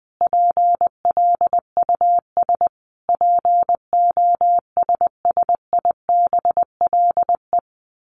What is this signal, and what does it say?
Morse code, heard as a keyed tone, about 20 wpm, 700 Hz. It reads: PLUS POSSIBLE
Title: A Sound Of Thunder 20wpm